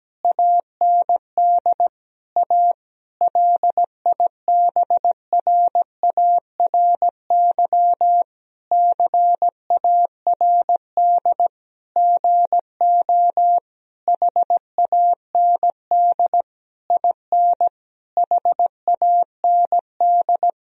Morse code, heard as a keyed tone, about 17 wpm, 700 Hz. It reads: AND A LIBRARY CARD GO HAND IN HAND
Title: Sentences from Top 1000 Words - Review Part 4 - 17wpm